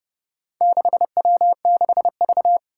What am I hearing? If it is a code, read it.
Morse code, heard as a keyed tone, about 28 wpm, 700 Hz. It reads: 6W6V
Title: Four Challenging Letter-Number Combinations 30wpm